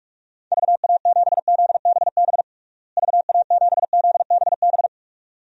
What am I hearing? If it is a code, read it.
Morse code, heard as a keyed tone, about 41 wpm, 700 Hz. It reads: VA7ZBB VA7ZBB
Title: DXCC-100 Call Signs - 6 Characters - Repeated 2x - 45wpm